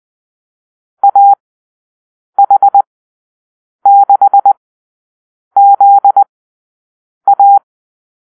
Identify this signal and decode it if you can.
Morse code, heard as a keyed tone, about 20 wpm, spaced as for 7 wpm, 800 Hz. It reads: AH6ZA
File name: arrl / Call-Sign Drill-7